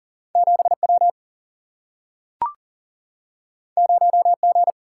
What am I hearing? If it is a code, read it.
Morse code, heard as a keyed tone, about 40 wpm, 700 Hz. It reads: 7 W 0 G
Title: Single-Letter Single-Number 40wpm